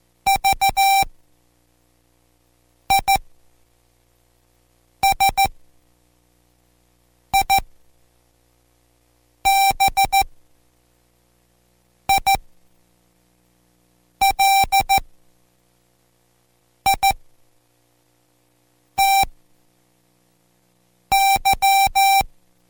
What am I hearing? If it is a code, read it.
Morse code, heard as a keyed tone, about 14 wpm, spaced as for 4 wpm, 780 Hz. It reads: VISIBILITY